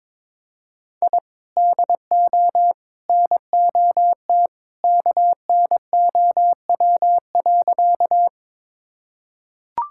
Morse code, heard as a keyed tone, about 22 wpm, 700 Hz. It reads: I DO NOT KNOW.
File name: practice / 22wpm / Sentences with 3-Word Phrases - 22wpm